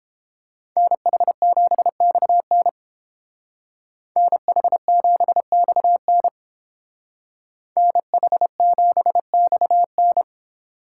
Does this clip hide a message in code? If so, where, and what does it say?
Yes: Morse code from 0.8 to 10.2 s, heard as a keyed tone, about 29 wpm, 700 Hz. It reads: NH7XN NH7XN NH7XN